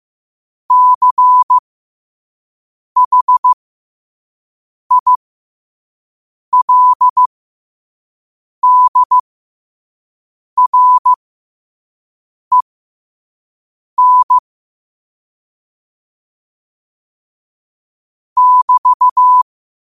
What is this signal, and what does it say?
Morse code, heard as a keyed tone, about 15 wpm, spaced as for 5 wpm, 1000 Hz. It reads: CHILDREN <BT>
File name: qotdfiles / 5wpm